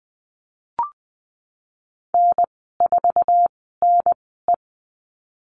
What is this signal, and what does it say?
Morse code, heard as a keyed tone, about 20 wpm, spaced as for 15 wpm, 700 Hz. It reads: N4NE
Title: 088 Lesson 15g - Number 5 - Call Signs 20wpm